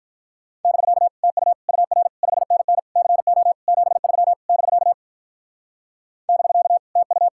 Morse code, heard as a keyed tone, about 43 wpm, 700 Hz. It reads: <BK> TU UR 5NN CO 73 <BK> <BK> TU
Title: Speed-Racing Common QSO Phrases - 35wpm